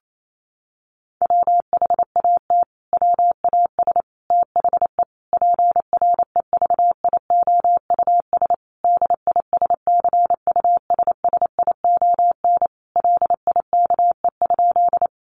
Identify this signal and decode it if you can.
Morse code, heard as a keyed tone, about 28 wpm, 700 Hz. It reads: WHAT WAS THE PREVIOUS DISCUSSION LIKE?
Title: Sentences from Top 1500 Words 28wpm